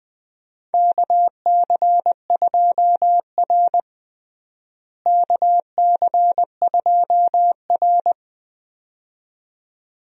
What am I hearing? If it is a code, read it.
Morse code, heard as a keyed tone, about 20 wpm, 700 Hz. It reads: KC2R KC2R
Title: USA Call Signs - 4 Characters - Repeated 2x - 20wpm